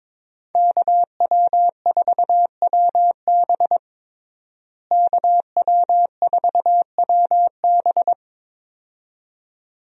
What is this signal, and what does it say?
Morse code, heard as a keyed tone, about 22 wpm, 700 Hz. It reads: KW4WB KW4WB
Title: USA Call Signs - 5 Characters - Repeated 2x - 22wpm